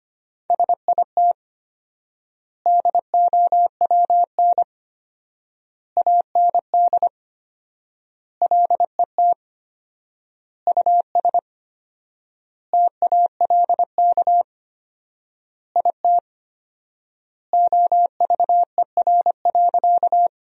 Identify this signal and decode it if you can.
Morse code, heard as a keyed tone, about 25 wpm, 700 Hz. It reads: SIT DOWN AND LET US TALK IT OVER.
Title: 4x Word-Spacing - Sentences from Top 500 Words - 25wpm